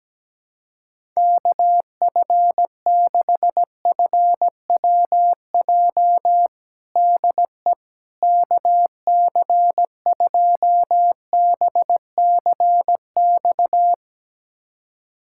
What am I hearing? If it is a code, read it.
Morse code, heard as a keyed tone, about 17 wpm, 700 Hz. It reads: KF6FWJ DE KC2BCX
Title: USA Callsign DE Callsign - 6 Characters - 17wpm